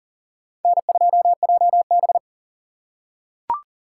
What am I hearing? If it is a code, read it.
Morse code, heard as a keyed tone, about 40 wpm, 700 Hz. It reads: N2JB
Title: USA Call Signs - 4 Characters - 4x Character Spacing - 40wpm